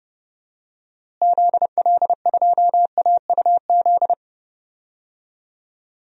Morse code, heard as a keyed tone, about 30 wpm, 700 Hz. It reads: ZL2AUZ
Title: DXCC-100 Call Signs - 6 Characters - Repeated 3x - 30wpm